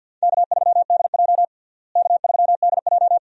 Morse code, heard as a keyed tone, about 52 wpm, 700 Hz. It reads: K3DJ K3DJ
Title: USA Call Signs - 4 Characters - Repeated 2x - 50wpm